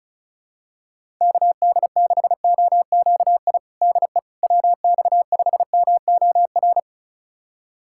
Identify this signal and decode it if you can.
Morse code, heard as a keyed tone, about 34 wpm, 700 Hz. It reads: KD6OQI DE WX5MOR
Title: USA Callsign DE Callsign - 6 Characters - 35wpm